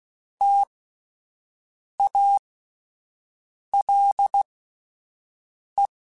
Morse code, heard as a keyed tone, about 16 wpm, spaced as for 5 wpm, 800 Hz. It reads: TALE
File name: T Prefixes / T mp3s